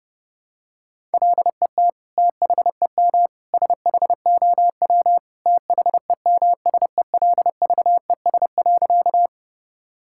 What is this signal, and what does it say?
Morse code, heard as a keyed tone, about 32 wpm, 700 Hz. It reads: LET THEM SHOW THEMSELVES.
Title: Sentences from Top 1000 Words - Review Part 5 - 30wpm